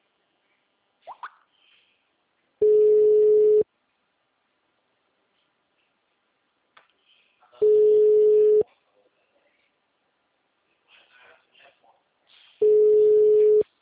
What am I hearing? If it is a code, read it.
Ringback tone (Europe, 425 Hz), with speech.